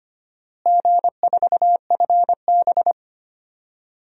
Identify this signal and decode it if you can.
Morse code, heard as a keyed tone, about 25 wpm, 700 Hz. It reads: G4FB
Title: DXCC-100 Call Signs - 4 Characters - Repeated 2x - 25wpm